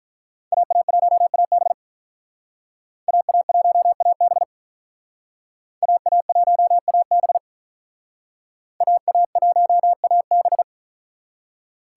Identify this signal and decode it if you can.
Morse code, heard as a keyed tone, about 45 wpm, 700 Hz. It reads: AA1AB AA1AB AA1AB AA1AB